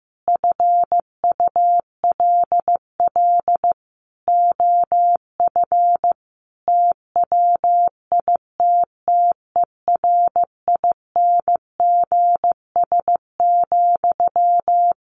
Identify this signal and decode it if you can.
Morse code, heard as a keyed tone, about 15 wpm, 700 Hz. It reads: FULL OF TWITTERINGS,
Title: A Sound Of Thunder 15wpm